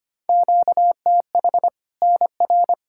Morse code, heard as a keyed tone, about 25 wpm, 700 Hz. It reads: QTH NR
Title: Practice QSO - Exchange 1 - Beginning - 25wpm